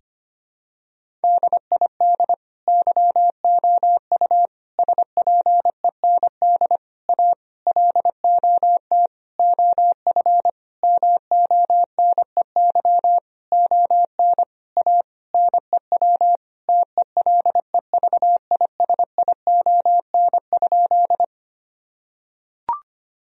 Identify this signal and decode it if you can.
Morse code, heard as a keyed tone, about 25 wpm, 700 Hz. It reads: DID YOU SPEND A LOT OF MONEY ON A NEW TELEVISION?
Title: Sentences from Top 1000 Words - Review Part 4 - 25wpm